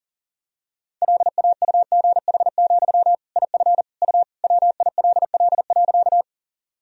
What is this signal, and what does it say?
Morse code, heard as a keyed tone, about 40 wpm, 700 Hz. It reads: LAUGH, IF U WILL.
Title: Sentences from Top 500 Words and Abbreviations 40wpm